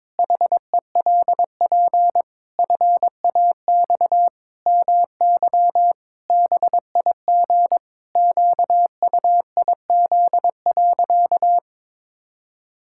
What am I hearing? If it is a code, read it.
Morse code, heard as a keyed tone, about 22 wpm, 700 Hz. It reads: HELP FAX MY BIG QUIZ.
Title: Pangram Challenge v2 - 22wpm